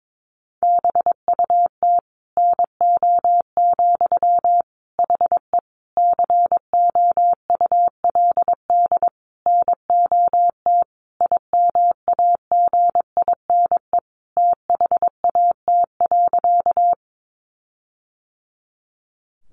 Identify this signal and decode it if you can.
Morse code, heard as a keyed tone, about 22 wpm, 700 Hz. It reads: BUT NO, HE COULD NOT IMAGINE THAT.
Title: Sentences from Top 1400 Words 22wpm